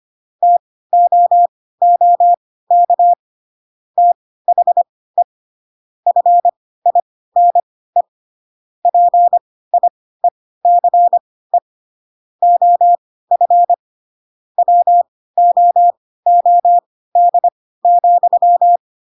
Morse code, heard as a keyed tone, about 25 wpm, spaced as for 16 wpm, 700 Hz. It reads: TOOK THE FINE PIECE OF WOOD,
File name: Pinocchio / Pinocchio ch2 10 25 ik1hge